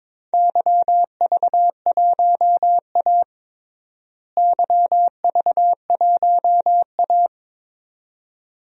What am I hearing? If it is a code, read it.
Morse code, heard as a keyed tone, about 21 wpm, 700 Hz. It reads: YV1A YV1A
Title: DXCC-100 Call Signs - 4 Characters - Repeated 2x - 22wpm